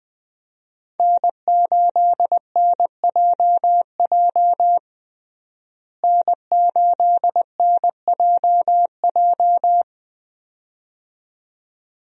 Morse code, heard as a keyed tone, about 20 wpm, 700 Hz. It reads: N8NJJ N8NJJ
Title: USA Call Signs - 5 Characters - Repeated 2x - 20wpm